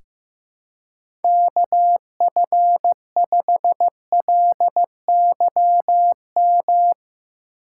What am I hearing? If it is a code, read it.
Morse code, heard as a keyed tone, about 15 wpm, 700 Hz. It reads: KF5LYM
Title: USA Call Signs - 6 Characters - 4x Character Spacing - 15wpm